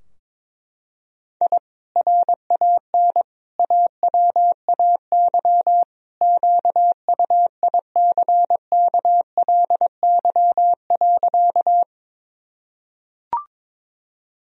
Morse code, heard as a keyed tone, about 22 wpm, 700 Hz. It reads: I RAN AWAY QUICKLY.
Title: Sentences from Top 1000 Words - Review Part 3 - 22wpm